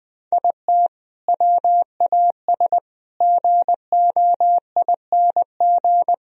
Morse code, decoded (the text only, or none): IT WAS GOING